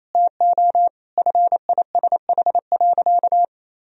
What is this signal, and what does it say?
Morse code, heard as a keyed tone, about 28 wpm, 700 Hz. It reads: TO FISH.
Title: Sentences from Top 1000 Words - Review Part 1 - 28wpm